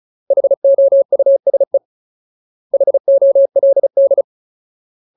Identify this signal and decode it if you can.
Morse code, heard as a keyed tone, about 36 wpm, 550 Hz. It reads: HOUSE HOLD
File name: Ultimate Word List / 35WPM Ultimate